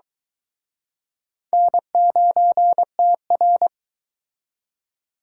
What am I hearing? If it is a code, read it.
Morse code, heard as a keyed tone, about 22 wpm, 700 Hz. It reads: N9TR